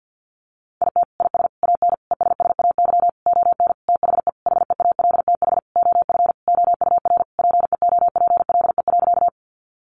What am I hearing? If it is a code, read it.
Morse code, heard as a keyed tone, about 50 wpm, 700 Hz. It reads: IT IS AN ESSAY ON THE HEALTH OF OUR PEOPLE.
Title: Sentences from Top 1500 Words 50wpm